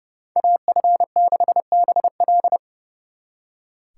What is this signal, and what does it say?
Morse code, heard as a keyed tone, about 36 wpm, 700 Hz. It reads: AF6BL
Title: Speed Racing - USA 2x2 Call Signs 30wpm